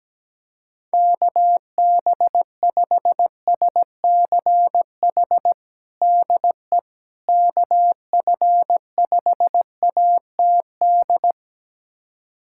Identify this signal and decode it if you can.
Morse code, heard as a keyed tone, about 17 wpm, 700 Hz. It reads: KB5SCH DE KF5ATD
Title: USA Callsign DE Callsign - 6 Characters - 17wpm